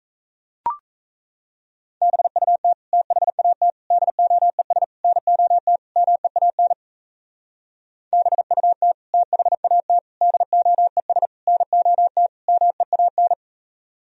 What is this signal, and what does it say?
Morse code, heard as a keyed tone, about 35 wpm, 700 Hz. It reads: BUT THAT DOES NOT MEAN BUT THAT DOES NOT MEAN
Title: Speed-Racing - Sets of 5 Words 28wpm